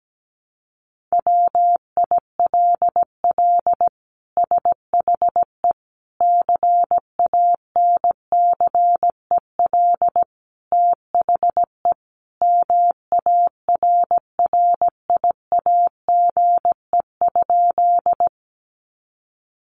Morse code, heard as a keyed tone, about 17 wpm, 700 Hz. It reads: WILL SHE CANCEL THE MARRIAGE?
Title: Sentences from Top 1200 Words 17wpm